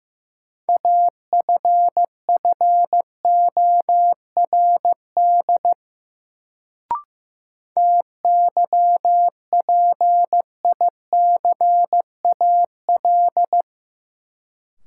Morse code, heard as a keyed tone, about 15 wpm, 700 Hz. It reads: AFFORD TYPICAL
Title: Sentences from Top 1100 Words - New Word Drill - 15wpm